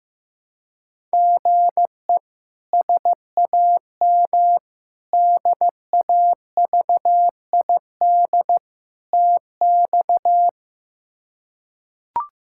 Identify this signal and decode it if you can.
Morse code, heard as a keyed tone, about 16 wpm, 700 Hz. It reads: GE SAM DAVID TX
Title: K1USN SST QSO Practice - 15wpm